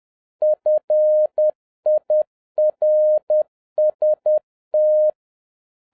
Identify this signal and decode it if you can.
Morse code, heard as a keyed tone, about 10 wpm, 610 Hz. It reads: FIRST